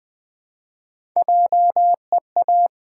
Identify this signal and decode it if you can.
Morse code, heard as a keyed tone, about 20 wpm, 700 Hz. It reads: JEA
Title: Build A Word - 1 Letter At A Time - 5-Letter Words - 20wpm